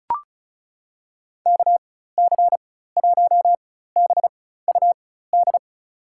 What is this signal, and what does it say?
Morse code, heard as a keyed tone, about 36 wpm, spaced as for 16 wpm, 700 Hz. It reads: KC1BUD
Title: USA Call Signs - 6 Characters - 4x Character Spacing - 35wpm